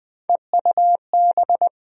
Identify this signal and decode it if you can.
Morse code, heard as a keyed tone, about 20 wpm, 700 Hz. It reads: EUB
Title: USA Call Sign Postfixes - 3 Letters 20wpm